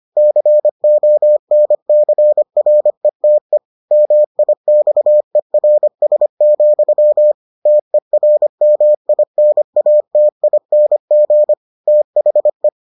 Morse code, heard as a keyed tone, about 25 wpm, 600 Hz. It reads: CONCRETE MIXERS, TERMINATING THE